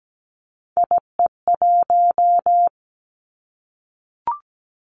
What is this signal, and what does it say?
Morse code, heard as a keyed tone, about 17 wpm, 700 Hz. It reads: IE1
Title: Three Challenging Letter-Number Combinations 17wpm